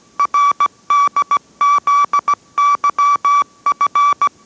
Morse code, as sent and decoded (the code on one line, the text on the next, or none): .-. -.. --.. -.-- ..-.
RDZYF